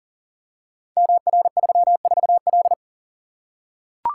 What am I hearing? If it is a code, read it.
Morse code, heard as a keyed tone, about 42 wpm, 700 Hz. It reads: MR34L
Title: Code Groups - 5 Letter:Numbers - 40wpm